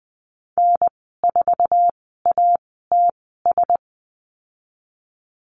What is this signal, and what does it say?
Morse code, heard as a keyed tone, about 20 wpm, spaced as for 15 wpm, 700 Hz. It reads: N4ATS
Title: 055 Lesson 10g - Letter r - Call Signs 20wpm